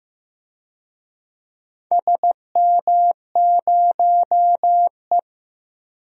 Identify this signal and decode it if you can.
Morse code, heard as a keyed tone, about 16 wpm, 700 Hz. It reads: SM0E
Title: DXCC-100 Call Signs - 4 Characters - Repeated 3x - 15wpm